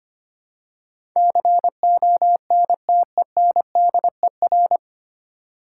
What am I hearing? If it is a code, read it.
Morse code, heard as a keyed tone, about 25 wpm, 690 Hz. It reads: CONTENDER
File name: practice / 25wpm / 9 Letter Words - First 3 Letters Given - 25wpm